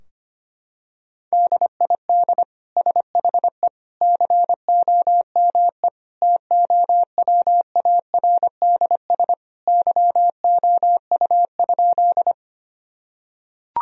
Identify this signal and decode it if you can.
Morse code, heard as a keyed tone, about 25 wpm, 700 Hz. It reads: DID SHE COME TOWARDS YOU?
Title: Sentences from Top 1000 Words - Review Part 2 - 25wpm